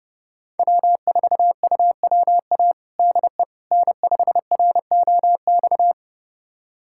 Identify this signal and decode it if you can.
Morse code, heard as a keyed tone, about 31 wpm, 700 Hz. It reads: W4UWA DE N5ROX
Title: USA Callsign DE Callsign - 5 Characters - 30wpm